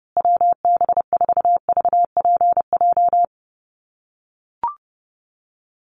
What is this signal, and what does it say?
Morse code, heard as a keyed tone, about 30 wpm, 700 Hz. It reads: WB4VPJ
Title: USA Call Signs Repeated 2x 30wpm